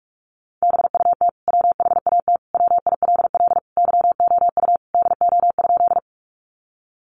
Morse code, heard as a keyed tone, about 44 wpm, 700 Hz. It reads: BUT WHAT WILL YOU DO?
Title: Sentences from Top 100 Words 45wpm